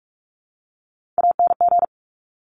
Morse code, heard as a keyed tone, about 46 wpm, 700 Hz. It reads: ANG
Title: Top Three Letter Combinations in Words 45wpm